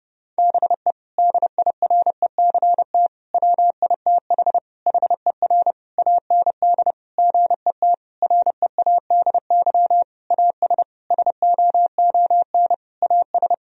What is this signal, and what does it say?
Morse code, heard as a keyed tone, about 30 wpm, 700 Hz. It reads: BE DIRECT WITH HER AND GET READY AS SOON AS
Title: Sentences from Top 1000 Words - Review Part 2 - 30wpm